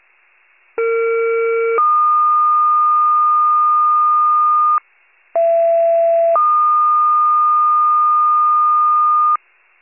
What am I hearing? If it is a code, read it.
Two-tone page, heard as a one-second tone, then a three-second tone, about 460 then 1220 Hz, about 670 then 1220 Hz. It is heard from 0.8 to 9.4 s.